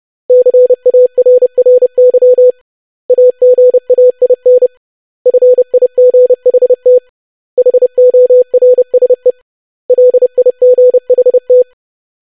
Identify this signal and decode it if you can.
Morse code, heard as a keyed tone, about 30 wpm, 500 Hz. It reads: CARRY AGAIN FIGHT HORSE LIGHT
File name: basic500-25wpm